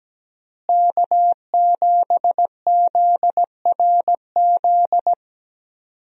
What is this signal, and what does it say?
Morse code, heard as a keyed tone, about 17 wpm, 700 Hz. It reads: K7ZRZ
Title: USA Call Signs - 5 Characters - Repeated 2x - 17wpm